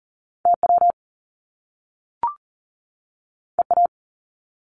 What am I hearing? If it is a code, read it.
Morse code, heard as a keyed tone, about 39 wpm, 700 Hz. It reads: TW EA
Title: Top Two Letter Combinations in Words 40wpm